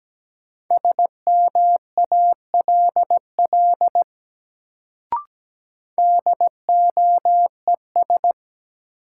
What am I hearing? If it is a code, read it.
Morse code, heard as a keyed tone, about 17 wpm, 700 Hz. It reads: SMALL DOES
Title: Sets of 2 Words - New Word Drill - 17wpm